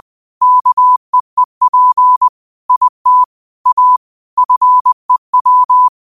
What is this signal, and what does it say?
Morse code, heard as a keyed tone, about 20 wpm, 1000 Hz. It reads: KEEP IT A FEW
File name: qotdfiles / 20wpm